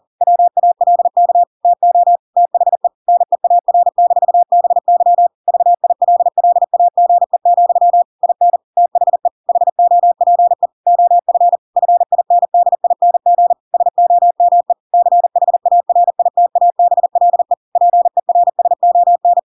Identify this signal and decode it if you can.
Morse code, heard as a keyed tone, about 40 wpm, 700 Hz. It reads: WALK TO THE NEAR-BY VILLAGE, IN THE HOPE OF FINDING SOME CHARITABLE PERSON
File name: Pinocchio / Pinocchio ch5 40 40 ik1hge